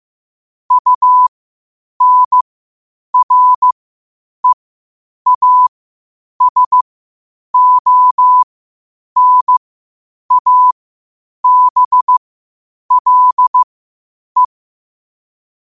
Morse code, heard as a keyed tone, about 15 wpm, spaced as for 8 wpm, 1000 Hz. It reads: UNREASONABLE
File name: qotdfiles / 8wpm